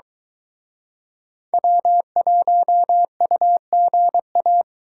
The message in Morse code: W1UGA

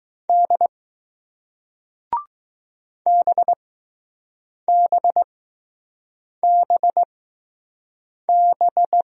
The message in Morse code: D BBBB

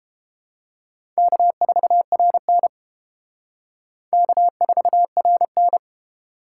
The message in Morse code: K4RN K4RN

